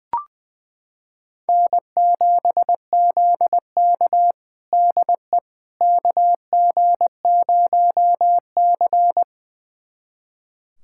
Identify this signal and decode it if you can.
Morse code, heard as a keyed tone, about 20 wpm, 700 Hz. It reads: N7ZK DE KG0C
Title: USA Callsign DE Callsign - 4 Characters - 20wpm